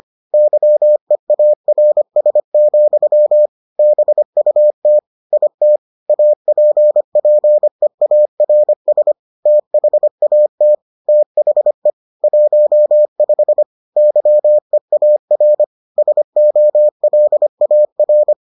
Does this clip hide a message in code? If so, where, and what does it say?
Yes: Morse code from 0.3 to 18.3 s, heard as a keyed tone, about 25 wpm, 600 Hz. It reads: YEARS, BUT IT APPEARS THAT THE 15 YEAR SOLAR